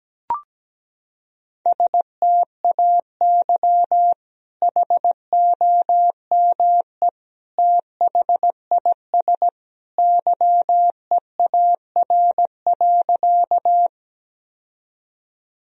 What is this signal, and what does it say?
Morse code, heard as a keyed tone, about 17 wpm, 700 Hz. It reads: STAY HOME THIS YEAR.
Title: Sentences from Top 1000 Words - Review Part 1 - 17wpm